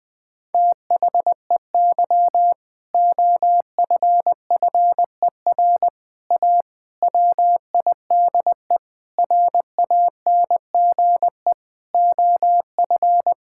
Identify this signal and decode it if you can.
Morse code, heard as a keyed tone, about 20 wpm, 700 Hz. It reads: THEY OFFER A WIDE RANGE OF
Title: Sentences with 4-Word Phrases - 20wpm